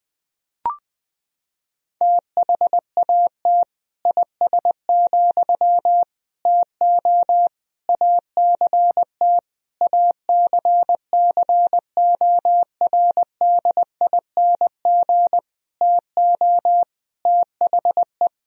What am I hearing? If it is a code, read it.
Morse code, heard as a keyed tone, about 20 wpm, 700 Hz. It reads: THAT IS, TO ACT ACCORDING TO THE
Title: Sentences from Top 1000 Words - Review Part 4 - 20wpm